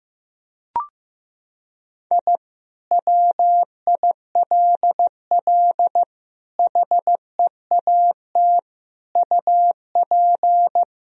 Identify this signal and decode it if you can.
Morse code, heard as a keyed tone, about 15 wpm, 700 Hz. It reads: I WILL HEAT UP
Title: Sentences from Top 1000 Words - Review Part 3 - 15wpm